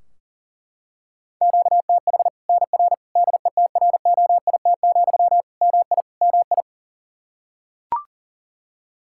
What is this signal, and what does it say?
Morse code, heard as a keyed tone, about 40 wpm, 700 Hz. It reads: QTH NR DETROIT, MI MI
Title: Practice QSO - Exchange 1 - Beginning - 40wpm